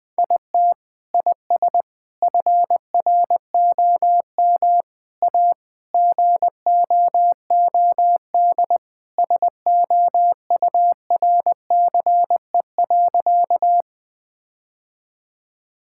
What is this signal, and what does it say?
Morse code, heard as a keyed tone, about 20 wpm, 700 Hz. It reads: IT IS FROM A GOOD SOURCE.
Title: Sentences from Top 1000 Words - Review Part 5 - 20wpm